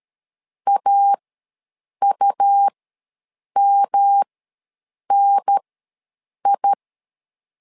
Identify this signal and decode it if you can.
Morse code, heard as a keyed tone, about 13 wpm, spaced as for 7 wpm, 800 Hz. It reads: AUMNI
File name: LEZIONI / 3 MWRO / 3 M ws 12wpm 1745